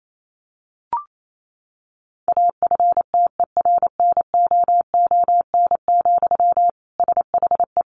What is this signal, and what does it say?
Morse code, heard as a keyed tone, about 28 wpm, 700 Hz. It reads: AFTERNOON, SHE
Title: Sentences from Top 1500 Words 28wpm